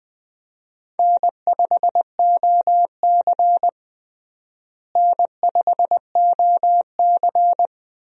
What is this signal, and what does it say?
Morse code, heard as a keyed tone, about 20 wpm, 700 Hz. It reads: N5OC N5OC
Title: USA Call Signs - 4 Characters - Repeated 2x - 20wpm